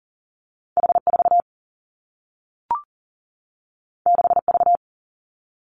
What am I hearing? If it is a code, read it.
Morse code, heard as a keyed tone, about 39 wpm, 700 Hz. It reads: H 4 6 V
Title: Two Challenging Letter-Number Combinations 40wpm